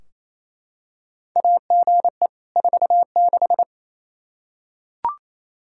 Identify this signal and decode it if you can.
Morse code, heard as a keyed tone, about 28 wpm, 700 Hz. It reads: AGE 46
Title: Practice QSO - Exchange 3 - More Info - 28wpm